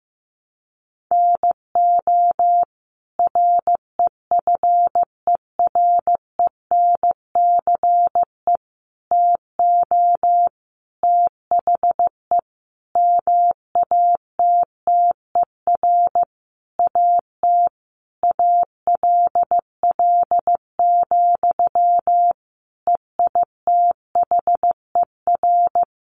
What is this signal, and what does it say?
Morse code, heard as a keyed tone, about 15 wpm, 700 Hz. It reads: NO REFERENCE TO THE MATTER AT ALL, EITHER
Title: Sentences from Top 1200 Words 15wpm